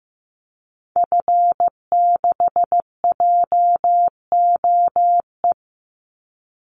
Morse code, heard as a keyed tone, about 15 wpm, 700 Hz. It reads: F6JOE
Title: Call Signs Repeated 3x - 15wpm